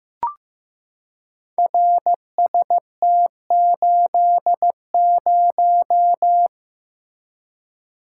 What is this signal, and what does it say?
Morse code, heard as a keyed tone, about 15 wpm, 700 Hz. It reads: RST80
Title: Code Groups - 5 Letter:Numbers - 15wpm